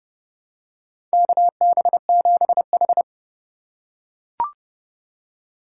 Morse code, heard as a keyed tone, about 32 wpm, 700 Hz. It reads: KB7H
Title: K1USN SST QSO Practice - 30wpm